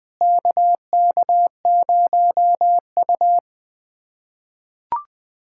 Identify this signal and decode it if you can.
Morse code, heard as a keyed tone, about 20 wpm, 700 Hz. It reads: KK0U
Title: USA Callsign DE Callsign - 4 Characters - 20wpm